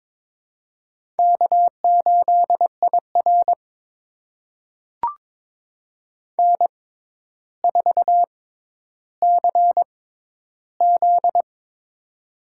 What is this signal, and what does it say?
Morse code, heard as a keyed tone, about 21 wpm, spaced as for 11 wpm, 700 Hz. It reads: K8IR N4CZ
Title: USA Call Signs - 4 Characters - 6x Character Spacing - 22wpm